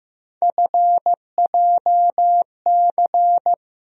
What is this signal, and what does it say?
Morse code, heard as a keyed tone, about 15 wpm, 700 Hz. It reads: FJC
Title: USA Call Sign Postfixes - 3 Letters 15wpm